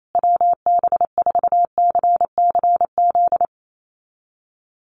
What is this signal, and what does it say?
Morse code, heard as a keyed tone, about 28 wpm, 700 Hz. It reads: WB4CCZ
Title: USA Call Signs Repeated 2x 28wpm